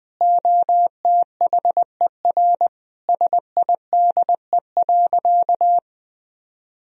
Morse code, heard as a keyed tone, about 20 wpm, 700 Hz. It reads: OTHER SIDE.
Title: Sentences from Top 1000 Words - Review Part 1 - 20wpm